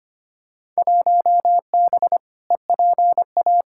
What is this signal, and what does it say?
Morse code, heard as a keyed tone, about 25 wpm, 700 Hz. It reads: 1B EPA
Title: Field Day Exchanges Encore - 25wpm